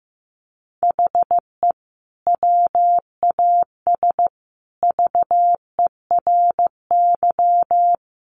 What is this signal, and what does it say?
Morse code, heard as a keyed tone, about 15 wpm, 700 Hz. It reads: HE WAS VERY
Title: Sentences from Top 1500 Words 15wpm